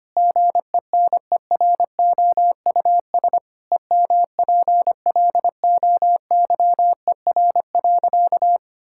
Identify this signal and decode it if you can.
Morse code, heard as a keyed tone, about 25 wpm, 700 Hz. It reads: GENEROUS EMPLOYER.
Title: Pangram Challenge v2 - 25wpm